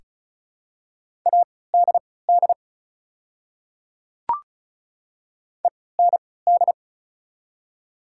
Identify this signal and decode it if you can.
Morse code, heard as a keyed tone, about 36 wpm, spaced as for 20 wpm, 700 Hz. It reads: ADD END
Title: 3x Character Spacing - Top 200 Words - 35wpm